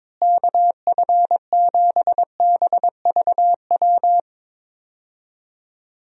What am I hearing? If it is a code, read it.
Morse code, heard as a keyed tone, about 22 wpm, 700 Hz. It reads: KF7BVW
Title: USA Call Signs - 6 Characters - Repeated 3x - 22wpm